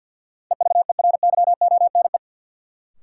Morse code, heard as a keyed tone, about 53 wpm, 700 Hz. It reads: EVERYONE
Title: Sentences from Top 800 Words - New Word Drill - 50wpm